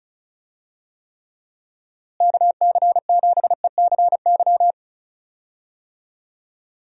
Morse code, heard as a keyed tone, about 36 wpm, 700 Hz. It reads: KC7ECY
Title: USA Call Signs - 6 Characters - Repeated 3x - 35wpm